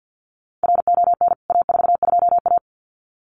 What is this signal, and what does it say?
Morse code, heard as a keyed tone, about 53 wpm, 700 Hz. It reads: RON A42A
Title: CWT Exchange 50wpm